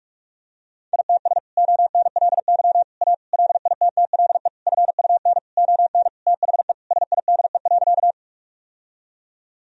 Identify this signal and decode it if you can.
Morse code, heard as a keyed tone, about 41 wpm, 700 Hz. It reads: ITS ONLY A LITTLE FUN ON THE SIDE.
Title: Sentences from Top 1000 Words - Review Part 4 - 45wpm